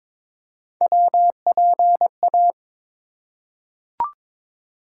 Morse code, heard as a keyed tone, about 22 wpm, 700 Hz. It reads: WPA
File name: ARRL Sections v2 - 22wpm